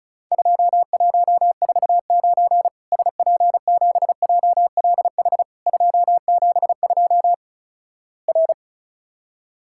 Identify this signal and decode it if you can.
Morse code, heard as a keyed tone, about 36 wpm, 700 Hz. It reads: 2149 SP7JLH 272 R
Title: Worked All Europe DX Contest - QTC Exchange - 35wpm